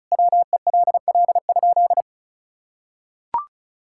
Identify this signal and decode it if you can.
Morse code, heard as a keyed tone, about 34 wpm, 700 Hz. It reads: WELL?
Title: Sentences from Top 1000 Words - Review Part 6 - 35wpm